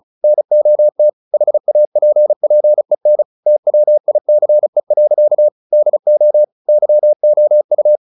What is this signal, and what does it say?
Morse code, heard as a keyed tone, about 34 wpm, 600 Hz. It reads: NOT HAPPEN TWICE. DO YOU
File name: the old man and the sea / the old man and the sea 1 35wpm